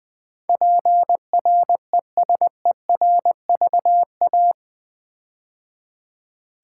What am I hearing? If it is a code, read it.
Morse code, heard as a keyed tone, about 20 wpm, 700 Hz. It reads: PRESERVA
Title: Build A Word - 4 Letters At A Time - 12-Letter Words - 20wpm